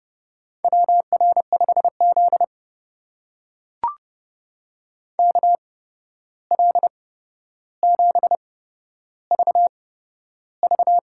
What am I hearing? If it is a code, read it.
Morse code, heard as a keyed tone, about 30 wpm, spaced as for 12 wpm, 700 Hz. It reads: WR5Z KL7VV
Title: USA Call Signs - 8x Character Spacing - 30wpm